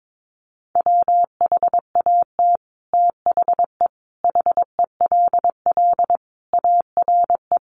Morse code, heard as a keyed tone, about 22 wpm, 700 Hz. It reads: WHAT THE HELL ARE
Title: Sets of 5 Words Encore 22wpm